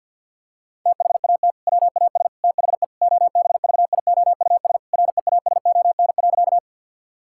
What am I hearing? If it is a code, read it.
Morse code, heard as a keyed tone, about 50 wpm, 700 Hz. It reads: THAT WAS THE OBVIOUS REASON.
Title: Sentences from Top 1600 Words 50wpm